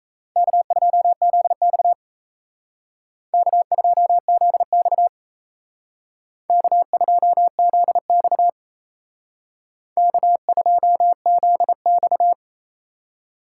Morse code, heard as a keyed tone, about 34 wpm, 700 Hz. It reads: K2ZX K2ZX K2ZX K2ZX
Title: Speed Racing - USA 1x2 Call Signs 28wpm